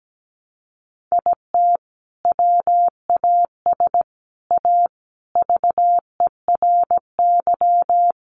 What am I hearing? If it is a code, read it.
Morse code, heard as a keyed tone, about 17 wpm, 700 Hz. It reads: IT WAS A VERY
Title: Sentences from Top 500 Words 17wpm